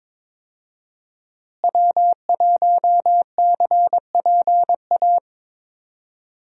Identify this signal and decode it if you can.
Morse code, heard as a keyed tone, about 22 wpm, 700 Hz. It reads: W1CPA
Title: USA Call Signs - 5 Characters - Repeated 2x - 22wpm